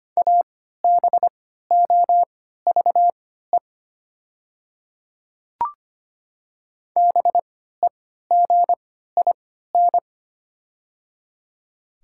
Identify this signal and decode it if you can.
Morse code, heard as a keyed tone, about 25 wpm, spaced as for 14 wpm, 700 Hz. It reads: ABOVE BEGIN
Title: 3x Character Spacing - Top 500 Words - 25wpm